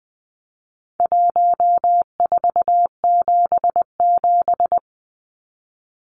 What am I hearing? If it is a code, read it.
Morse code, heard as a keyed tone, about 20 wpm, 700 Hz. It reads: 1477
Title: Numbers - Contest Style - 4-Digit 20wpm